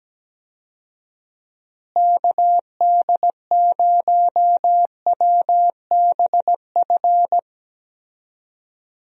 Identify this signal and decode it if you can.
Morse code, heard as a keyed tone, about 17 wpm, 700 Hz. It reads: KD0WBF
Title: USA Call Signs - 6 Characters - Repeated 3x - 17wpm